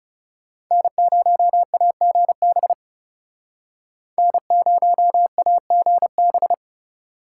Morse code, heard as a keyed tone, about 36 wpm, 700 Hz. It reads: N0AGB N0AGB
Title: Speed Racing - USA 1x3 Call Signs 30wpm